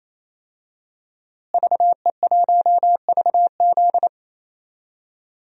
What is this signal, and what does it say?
Morse code, heard as a keyed tone, about 28 wpm, 700 Hz. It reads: VE1VZ
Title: DXCC-100 Call Signs - 5 Characters - Repeated 3x - 28wpm